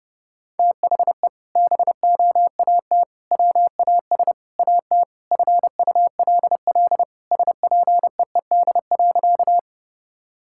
Morse code, heard as a keyed tone, about 32 wpm, 700 Hz. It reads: THE BOAT WAS AT FULL SPEED.
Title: Sentences from Top 1000 Words - Review Part 5 - 30wpm